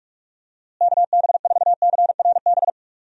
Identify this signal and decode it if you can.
Morse code, heard as a keyed tone, about 46 wpm, 700 Hz. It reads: KB4CRB
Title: USA Call Signs - 6 Characters - Repeated 2x - 45wpm